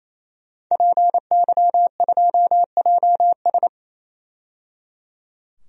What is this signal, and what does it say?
Morse code, heard as a keyed tone, about 28 wpm, 700 Hz. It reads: PY2JS
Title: DXCC-100 Call Signs - 5 Characters - 28wpm